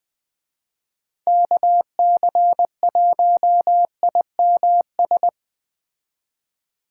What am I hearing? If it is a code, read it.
Morse code, heard as a keyed tone, about 20 wpm, 700 Hz. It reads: KC1IMS